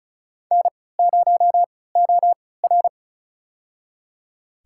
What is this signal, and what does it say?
Morse code, heard as a keyed tone, about 34 wpm, spaced as for 20 wpm, 700 Hz. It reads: N0OR
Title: USA Call Signs - 4 Characters - 3x Character Spacing - 35wpm